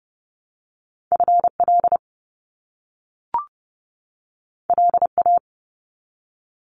Morse code, heard as a keyed tone, about 30 wpm, 700 Hz. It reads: FL LA
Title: US States 30wpm